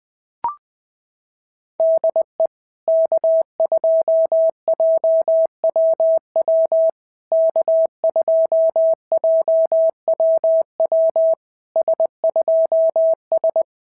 Morse code, heard as a keyed tone, about 20 wpm, 650 Hz. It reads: DE K2JWW K2JWW S2S
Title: SOTA QSO Practice 20wpm